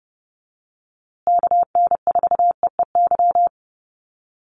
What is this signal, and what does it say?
Morse code, heard as a keyed tone, about 30 wpm, 700 Hz. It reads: KN4EEY
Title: USA Call Signs Repeated 3x 30wpm